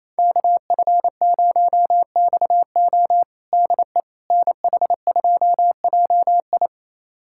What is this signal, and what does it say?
Morse code, heard as a keyed tone, about 28 wpm, 700 Hz. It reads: KF0XO DE NH2JI
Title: USA Callsign DE Callsign - 5 Characters - 28wpm